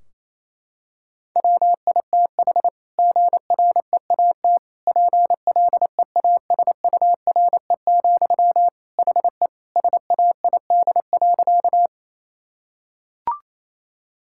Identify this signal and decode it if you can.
Morse code, heard as a keyed tone, about 28 wpm, 700 Hz. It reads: WITH GREAT PLEASURE, HE SAID.
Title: Sentences from Top 1600 Words 28wpm